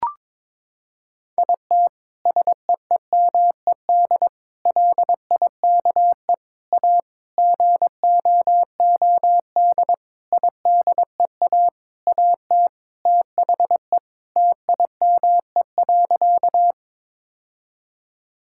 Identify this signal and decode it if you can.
Morse code, heard as a keyed tone, about 21 wpm, 700 Hz. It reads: IT SEEMED LIKE A GOOD IDEA AT THE TIME.
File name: Sentences with 9-Word Phrases - 22wpm